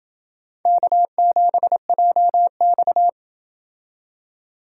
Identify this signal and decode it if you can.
Morse code, heard as a keyed tone, about 28 wpm, 700 Hz. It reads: K7JX